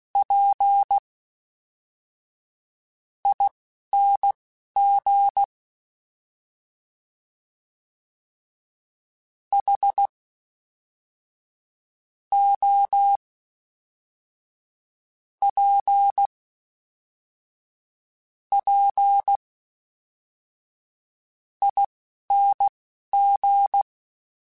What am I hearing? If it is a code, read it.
Morse code, heard as a keyed tone, about 16 wpm, spaced as for 5 wpm, 800 Hz. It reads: PING HOPPING